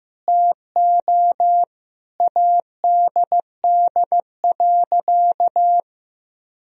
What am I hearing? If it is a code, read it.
Morse code, heard as a keyed tone, about 16 wpm, 700 Hz. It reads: TO ADD.
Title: Sentences from Top 1000 Words - Review Part 1 - 15wpm